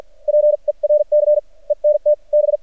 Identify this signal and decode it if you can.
Morse code, heard as a keyed tone, about 50 wpm, spaced as for 36 wpm, 600 Hz. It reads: PERCENT B
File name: archived / CRSnet